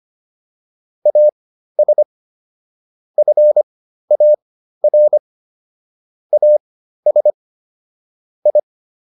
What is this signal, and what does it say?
Morse code, heard as a keyed tone, about 25 wpm, spaced as for 13 wpm, 600 Hz. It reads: AS FAR AS I